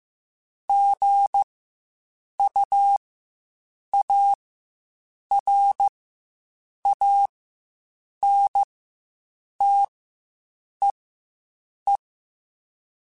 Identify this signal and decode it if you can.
Morse code, heard as a keyed tone, about 15 wpm, spaced as for 7 wpm, 800 Hz. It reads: GUARANTEE
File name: Quotes Spaced / Quotes Spaced-mp3s / Airtight sp @10